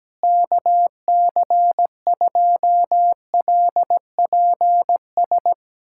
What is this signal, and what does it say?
Morse code, heard as a keyed tone, about 17 wpm, 700 Hz. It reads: KC2LPS
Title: USA Callsign DE Callsign 17wpm